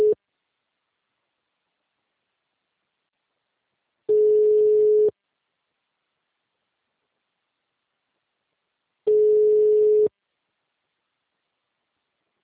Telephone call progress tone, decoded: ringback tone

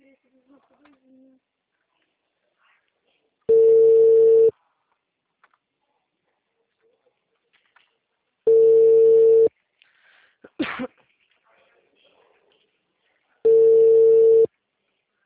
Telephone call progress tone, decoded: ringback tone

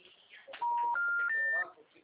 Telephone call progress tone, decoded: special information tone

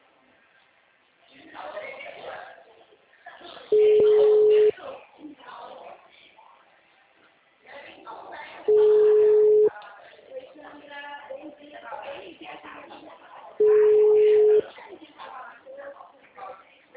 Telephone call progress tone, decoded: ringback tone

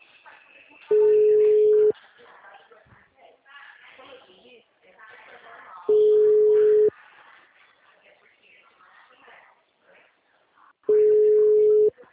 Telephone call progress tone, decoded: ringback tone

